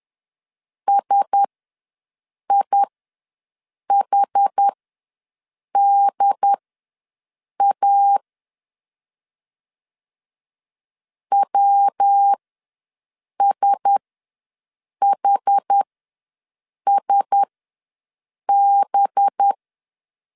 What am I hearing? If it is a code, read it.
Morse code, heard as a keyed tone, about 10 wpm, spaced as for 6 wpm, 800 Hz. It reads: SIHDA WSHSB
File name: LEZIONI / 5 LBVC / 5 LB ws 10wpm 3616